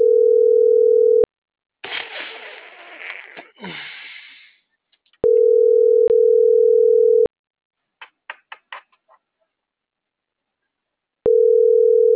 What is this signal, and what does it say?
Ringback tone (North America), 440 + 480 Hz.